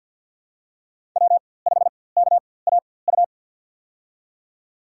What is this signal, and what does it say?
Morse code, heard as a keyed tone, about 53 wpm, spaced as for 24 wpm, 700 Hz. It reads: W5KAU
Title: USA Call Signs - 5 Characters - 4x Character Spacing - 50wpm